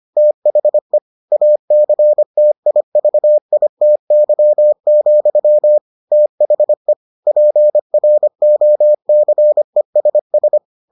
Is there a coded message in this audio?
Yes: Morse code from 0.2 to 10.6 s, heard as a keyed tone, about 25 wpm, 600 Hz. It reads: THE ACTIVITY, THE PROCESS